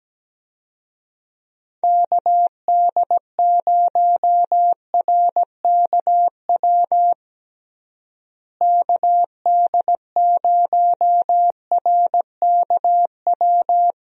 Morse code, heard as a keyed tone, about 17 wpm, 700 Hz. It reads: KD0RKW KD0RKW
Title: USA Call Signs - 6 Characters - Repeated 3x - 17wpm